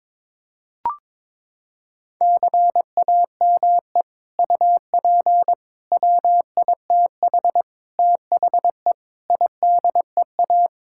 Morse code, heard as a keyed tone, about 22 wpm, 700 Hz. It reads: CAME UP WITH THE IDEA
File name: practice / 22wpm / Sets of 6 Words - 22wpm